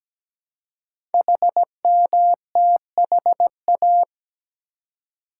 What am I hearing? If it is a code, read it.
Morse code, heard as a keyed tone, about 17 wpm, 700 Hz. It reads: HMTHA
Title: Code Groups - 5 Letters - 17wpm